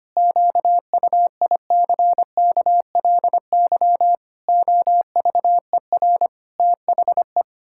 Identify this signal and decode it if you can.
Morse code, heard as a keyed tone, about 25 wpm, 700 Hz. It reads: QUICKLY OVER THE
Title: Pangram Challenge v2 - 25wpm